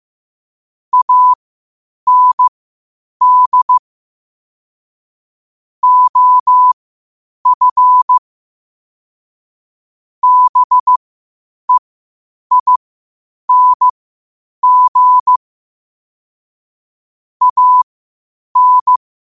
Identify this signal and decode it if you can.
Morse code, heard as a keyed tone, about 15 wpm, spaced as for 8 wpm, 1000 Hz. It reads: AND OF BEING AN
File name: qotdfiles / 8wpm